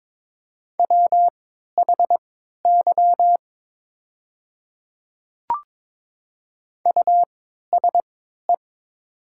Morse code, heard as a keyed tone, about 22 wpm, spaced as for 12 wpm, 700 Hz. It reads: WHY USE
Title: 3x Character Spacing - Top 200 Words - 22wpm